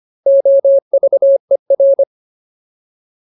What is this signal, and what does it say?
Morse code, heard as a keyed tone, about 25 wpm, 550 Hz. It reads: OVER
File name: Ultimate Word List / 25WPM Ultimate